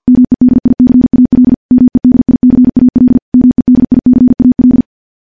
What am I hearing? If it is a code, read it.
Morse code, heard as a keyed tone, about 50 wpm, 250 Hz. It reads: MEDICAL MEDICAL MEDICAL
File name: morse / morse-training-50WPM